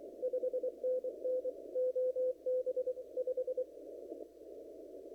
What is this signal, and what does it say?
Morse code, heard as a keyed tone, about 25 wpm, 510 Hz. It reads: 5NN OB 5
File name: CW / kt5j wpx cw-op-k5pi